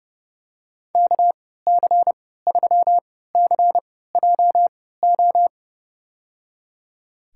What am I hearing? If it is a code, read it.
Morse code, heard as a keyed tone, about 31 wpm, spaced as for 17 wpm, 700 Hz. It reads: KC3CJO
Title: USA Call Signs - 6 Characters - 3x Character Spacing - 30wpm